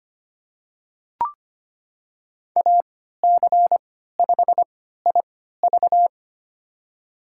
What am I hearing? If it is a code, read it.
Morse code, heard as a keyed tone, about 25 wpm, spaced as for 14 wpm, 700 Hz. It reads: AC5IV
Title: USA Call Signs - 5 Characters - 3x Character Spacing - 25wpm